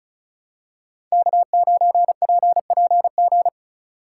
Morse code, heard as a keyed tone, about 34 wpm, 700 Hz. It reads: K9PPG
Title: USA Call Signs - 3x Character Spacing - 35wpm